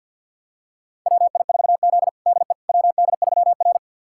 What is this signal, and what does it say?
Morse code, heard as a keyed tone, about 52 wpm, 700 Hz. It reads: WI4Z DE WD3R
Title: USA Callsign DE Callsign - 4 Characters - 50wpm